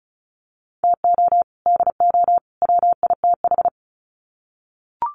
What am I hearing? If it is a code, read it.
Morse code, heard as a keyed tone, about 34 wpm, 700 Hz. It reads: TO DO WITH
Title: Sets of 3 Words 35wpm